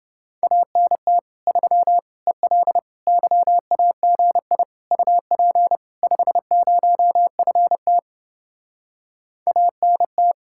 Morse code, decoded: ANT 3 EL YAGI UP 50FT ANT

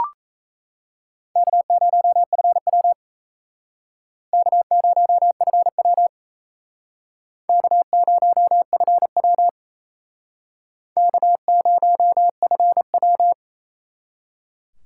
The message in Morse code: K0FW K0FW K0FW K0FW

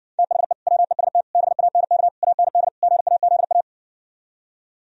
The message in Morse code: THE WEST BANK AND GAZA

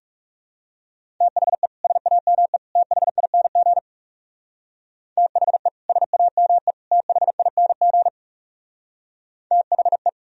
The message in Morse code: THE SAME THING THE SAME THING THE

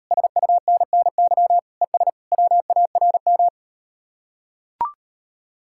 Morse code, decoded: SUNNY ES WARM

